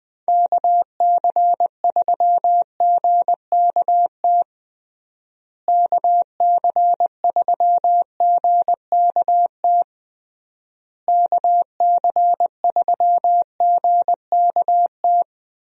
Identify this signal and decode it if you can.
Morse code, heard as a keyed tone, about 20 wpm, 700 Hz. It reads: KC3GKT KC3GKT KC3GKT
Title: USA Call Signs - 6 Characters - Repeated 3x - 20wpm